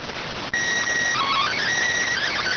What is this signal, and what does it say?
SSTV (slow-scan television), PD 290.